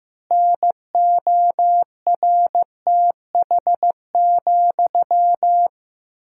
Morse code, heard as a keyed tone, about 16 wpm, 700 Hz. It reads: NORTH,
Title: 2x Word-Spacing - Sentences from Top 400 Words - 15wpm